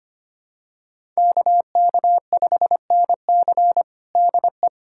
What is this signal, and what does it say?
Morse code, heard as a keyed tone, about 25 wpm, 700 Hz. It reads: KK5NC DE
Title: USA Callsign DE Callsign 25wpm